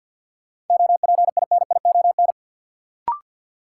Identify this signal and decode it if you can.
Morse code, heard as a keyed tone, about 45 wpm, 700 Hz. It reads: OPINION
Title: Sentences from Top 1200 Words - New Word Drill Alt - 50wpm